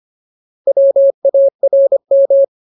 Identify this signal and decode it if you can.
Morse code, heard as a keyed tone, about 25 wpm, 550 Hz. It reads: WARM
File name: Ultimate Word List / 25WPM Ultimate